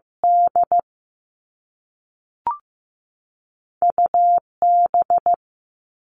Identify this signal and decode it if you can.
Morse code, heard as a keyed tone, about 15 wpm, spaced as for 3 wpm, 700 Hz. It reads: DU B
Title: Two Challenging Letter-Number Combinations 15wpm